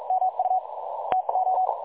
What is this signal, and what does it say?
Morse code, heard as a keyed tone, about 46 wpm, spaced as for 38 wpm, 750 Hz, with noise. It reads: MU TO
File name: dlazaawansowanych / nagraniazpasma / websdr recording 3565.0kHz